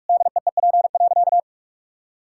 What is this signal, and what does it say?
Morse code, heard as a keyed tone, about 46 wpm, 700 Hz. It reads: DEEP.
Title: Sentences from Top 1600 Words 45wpm